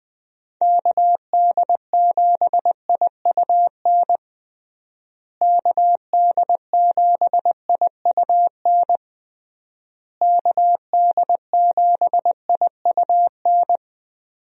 Morse code, heard as a keyed tone, about 21 wpm, 700 Hz. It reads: KD7IUN KD7IUN KD7IUN
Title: USA Call Signs - 6 Characters - Repeated 3x - 20wpm